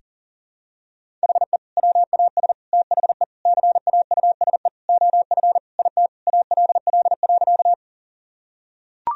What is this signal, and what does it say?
Morse code, heard as a keyed tone, about 40 wpm, 700 Hz. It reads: HE WAS THE CAUSE OF IT ALL.
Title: Sentences from Top 1000 Words - Review Part 1 - 40wpm